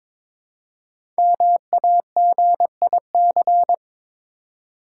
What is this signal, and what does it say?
Morse code, heard as a keyed tone, about 22 wpm, 700 Hz. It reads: MAGIC
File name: 5 Letter Words - First 3 Letters Given - 22wpm